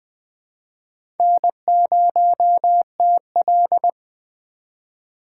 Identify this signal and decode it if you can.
Morse code, heard as a keyed tone, about 20 wpm, 700 Hz. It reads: N0TL
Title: USA Call Signs - 4x Character Spacing - 20wpm